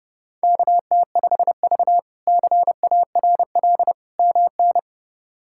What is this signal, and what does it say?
Morse code, heard as a keyed tone, about 30 wpm, 700 Hz. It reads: KT5V CARL MN
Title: K1USN SST QSO Practice - 30wpm